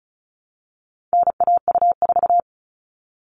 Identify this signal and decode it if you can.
Morse code, heard as a keyed tone, about 34 wpm, 700 Hz. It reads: NAU4
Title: All Cut Numbers - 4-Digit 35wpm